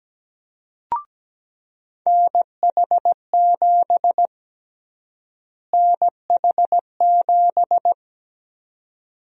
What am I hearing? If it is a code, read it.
Morse code, heard as a keyed tone, about 17 wpm, 700 Hz. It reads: NH7 NH7
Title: USA Call Sign Prefixes - Repeated 2x - 17wpm